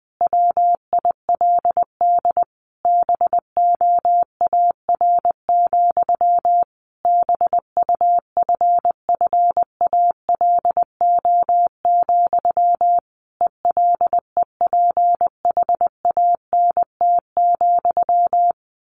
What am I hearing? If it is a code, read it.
Morse code, heard as a keyed tone, about 20 wpm, 700 Hz. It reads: WILD BOAR, BUFFALO, ELEPHANT,
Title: A Sound Of Thunder 20wpm